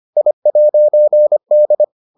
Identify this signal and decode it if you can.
Morse code, heard as a keyed tone, about 25 wpm, 600 Hz. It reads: I'D